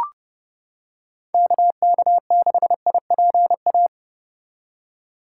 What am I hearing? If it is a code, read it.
Morse code, heard as a keyed tone, about 29 wpm, 700 Hz. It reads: KK6IPA